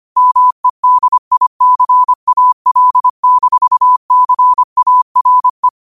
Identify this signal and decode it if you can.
Morse code, heard as a keyed tone, about 25 wpm, 1000 Hz. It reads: MEDICAL-CARE
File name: qotdfiles / 25wpm